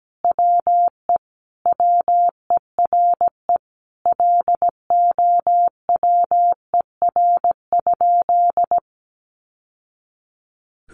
Morse code, heard as a keyed tone, about 17 wpm, 700 Hz. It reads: WE WERE LOWER?
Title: Sentences from Top 700 Words 17wpm